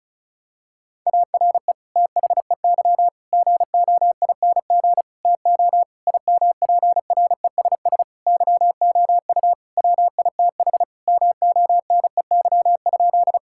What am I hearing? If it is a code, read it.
Morse code, heard as a keyed tone, about 34 wpm, 700 Hz. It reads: ARE THEY GOING TO IMPRESS YOU WITH MONEY?
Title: Sentences from Top 1600 Words 35wpm